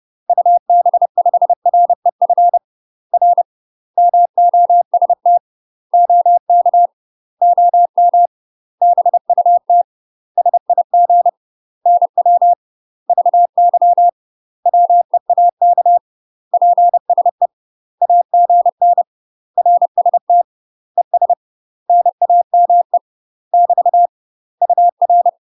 Morse code, heard as a keyed tone, about 30 wpm, 700 Hz. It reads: UB5REF R MOST OK OM BUT SIG NW VY WEAK PSE AGN RST ES NAME <BT> UR
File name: UK MASTER QSOs TRAINING / 30WPM / UK MASTER QSOs 0044